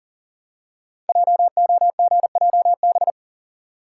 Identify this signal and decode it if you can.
Morse code, heard as a keyed tone, about 40 wpm, 700 Hz. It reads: JOGJB
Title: Code Groups - 5 Letters - 40wpm